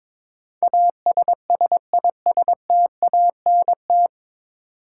Morse code, heard as a keyed tone, about 22 wpm, 700 Hz. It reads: ASSISTANT
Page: Morse Code Ninja practice 22wpm